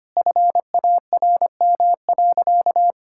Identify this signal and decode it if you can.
Morse code, heard as a keyed tone, about 25 wpm, 700 Hz. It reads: FARM.
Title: Sentences from Top 1000 Words - Review Part 5 - 25wpm